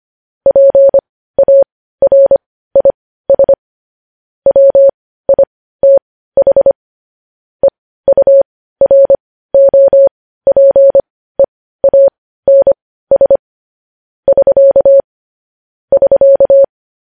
Morse code, heard as a keyed tone, about 25 wpm, spaced as for 15 wpm, 550 Hz. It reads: PARIS WITH EUROPEANS <SK> <SK>
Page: news-headlines-in-morse-code-at-15-wpm